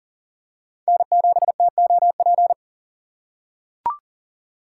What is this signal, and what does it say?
Morse code, heard as a keyed tone, about 40 wpm, 700 Hz. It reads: N7TOP
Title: USA Call Signs - 5 Characters - 3x Character Spacing - 40wpm